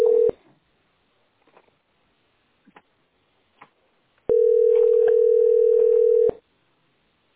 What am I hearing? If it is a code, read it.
Ringback tone (North America), 440 + 480 Hz.